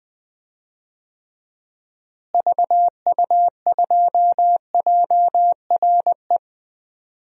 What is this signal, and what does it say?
Morse code, heard as a keyed tone, about 20 wpm, 700 Hz. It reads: VU2JRE
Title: DXCC-100 Call Signs - 6 Characters - Repeated 3x - 20wpm